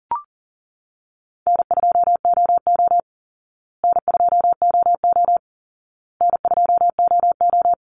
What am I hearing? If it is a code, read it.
Morse code, heard as a keyed tone, about 40 wpm, 700 Hz. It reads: N2OO N2OO N2OO
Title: Easy Call Signs Repeated 3x - 40wpm